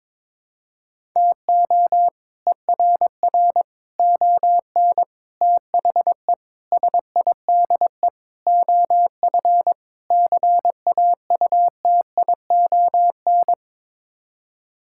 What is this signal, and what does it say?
Morse code, heard as a keyed tone, about 22 wpm, 700 Hz. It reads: TO ERR ON THE SIDE OF CAUTION
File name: practice / 22wpm / Sets of 7 Words - 22wpm